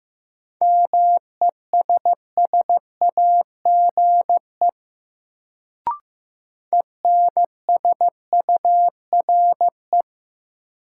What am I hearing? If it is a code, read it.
Morse code, heard as a keyed tone, about 15 wpm, 700 Hz. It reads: MESSAGE ENSURE
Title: Sentences from Top 1100 Words - New Word Drill Alt - 15wpm